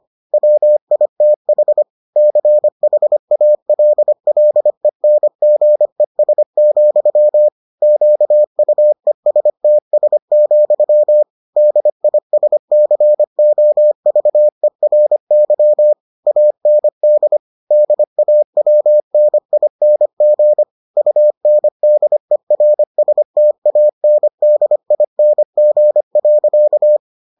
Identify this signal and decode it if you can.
Morse code, heard as a keyed tone, about 25 wpm, 600 Hz. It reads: WITH CHALLENGES, QUESTS, DISCOVERY AND DAWNING UNDERSTANDING.